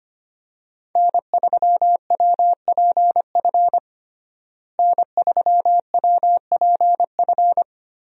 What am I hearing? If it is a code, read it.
Morse code, heard as a keyed tone, about 25 wpm, 700 Hz. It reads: N3WPF N3WPF
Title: USA Call Signs - 5 Characters - Repeated 3x - 25wpm